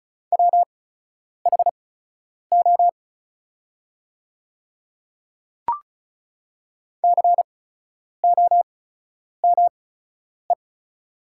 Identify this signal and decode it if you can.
Morse code, heard as a keyed tone, about 34 wpm, spaced as for 10 wpm, 700 Hz. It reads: WHO COME
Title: 8x Character Spacing - Top 100 Words - 35wpm